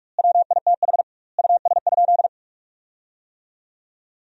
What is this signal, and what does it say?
Morse code, heard as a keyed tone, about 43 wpm, 700 Hz. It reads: WITH US?
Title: 2x Word-Spacing - Sentences from Top 200 Words - 45wpm